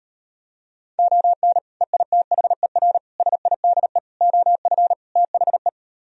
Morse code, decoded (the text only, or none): ON EITHER SIDE OF THE